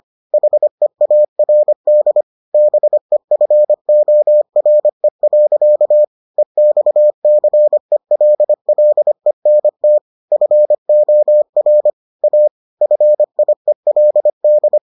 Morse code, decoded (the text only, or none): HEARD BEFORE. EXCELLENT FOR A FIELD